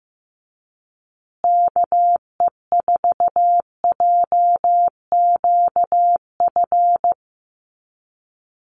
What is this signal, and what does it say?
Morse code, heard as a keyed tone, about 15 wpm, 700 Hz. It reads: KE4JQF
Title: USA Call Signs Repeated 3x 15wpm